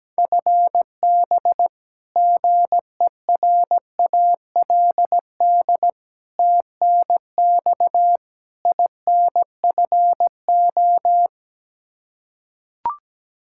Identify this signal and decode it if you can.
Morse code, heard as a keyed tone, about 17 wpm, 700 Hz. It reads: FB GERALD TNX INFO
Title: Practice QSO - Exchange 3 - More Info - 17wpm